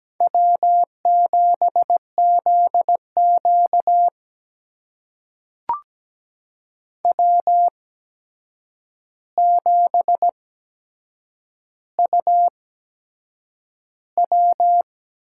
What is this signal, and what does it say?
Morse code, heard as a keyed tone, about 17 wpm, spaced as for 7 wpm, 700 Hz. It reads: W7ZQ W7UW
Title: USA Call Signs - 4 Characters - 8x Character Spacing - 17wpm